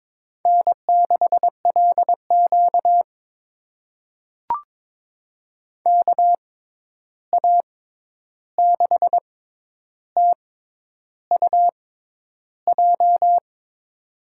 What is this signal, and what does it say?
Morse code, heard as a keyed tone, about 22 wpm, spaced as for 10 wpm, 700 Hz. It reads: N6LQ KA6TUJ
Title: USA Call Signs - 6x Character Spacing - 22wpm